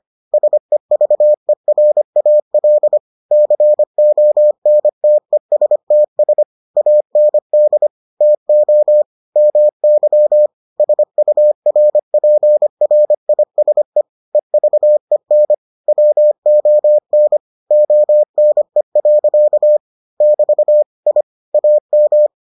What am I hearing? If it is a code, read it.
Morse code, heard as a keyed tone, about 25 wpm, 600 Hz. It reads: SEVERAL CONTESTS AND TO MY SURPRISE EVEN WON ONE. <BT> I AM